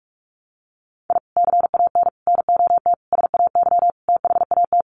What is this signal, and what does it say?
Morse code, heard as a keyed tone, about 44 wpm, 700 Hz. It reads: I CAN NOT SAY THAT
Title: Sets of 5 Words Encore2 45wpm